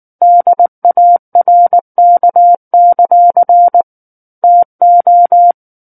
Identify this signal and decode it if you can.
Morse code, heard as a keyed tone, about 19 wpm, 700 Hz. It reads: DARK; TO
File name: Book One / 10-Chapter 10 - 19 WPM